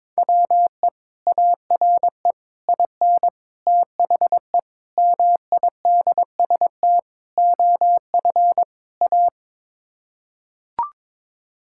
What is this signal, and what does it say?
Morse code, heard as a keyed tone, about 22 wpm, 700 Hz. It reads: WE ARE IN THE MIDST OF A
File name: practice / 22wpm / Sets of 7 Words - 22wpm